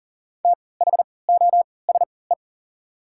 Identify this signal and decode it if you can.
Morse code, heard as a keyed tone, about 40 wpm, spaced as for 23 wpm, 700 Hz. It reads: THOSE
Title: 3x Character Spacing - Top 500 Words - 40wpm